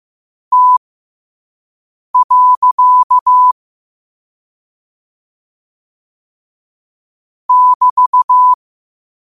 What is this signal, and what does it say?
Morse code, heard as a keyed tone, about 15 wpm, spaced as for 5 wpm, 1000 Hz. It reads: T. <BT>
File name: qotdfiles / 5wpm